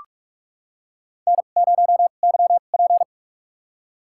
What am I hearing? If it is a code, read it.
Morse code, heard as a keyed tone, about 43 wpm, spaced as for 31 wpm, 700 Hz. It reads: N0YP
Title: USA Call Signs - 4 Characters - 2x Character Spacing - 45wpm